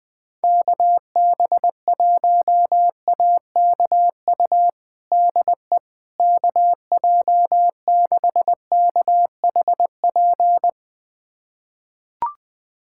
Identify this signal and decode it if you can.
Morse code, heard as a keyed tone, about 21 wpm, 700 Hz. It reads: KB1AKU DE KJ6KHP
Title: USA Callsign DE Callsign - 6 Characters - 20wpm